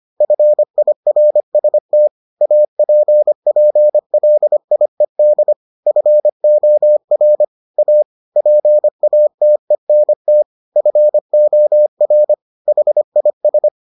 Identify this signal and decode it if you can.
Morse code, heard as a keyed tone, about 25 wpm, 600 Hz. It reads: FIRST APPLIED FOR A PATENT FOR HIS